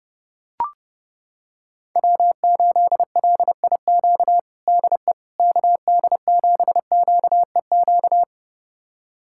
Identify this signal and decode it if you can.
Morse code, heard as a keyed tone, about 30 wpm, 700 Hz. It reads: W8LIQ DE KD7QEQ
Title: USA Callsign DE Callsign 30wpm